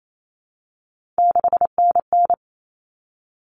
Morse code, heard as a keyed tone, about 28 wpm, 700 Hz. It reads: 6NN
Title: Common Cut Numbers - 3-Digit 28wpm